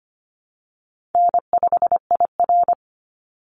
Morse code, heard as a keyed tone, about 25 wpm, 700 Hz. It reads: N5IR
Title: Call Signs - 25wpm